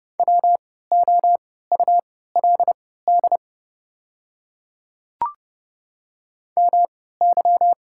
Morse code, heard as a keyed tone, about 30 wpm, spaced as for 17 wpm, 700 Hz. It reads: WOULD MY
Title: 3x Character Spacing - Top 100 Words - 30wpm